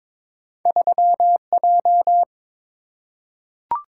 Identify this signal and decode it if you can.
Morse code, heard as a keyed tone, about 22 wpm, 700 Hz. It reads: 3J
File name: practice / 22wpm / Two Letter-Number Combinations - 22wpm